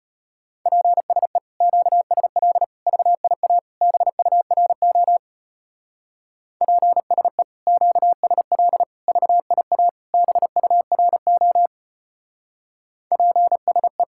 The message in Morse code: PSE QSL VIA BURO PSE QSL VIA BURO PSE

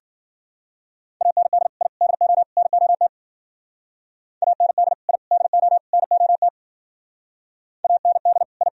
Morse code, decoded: AND I DO NOT AND I DO NOT AND I

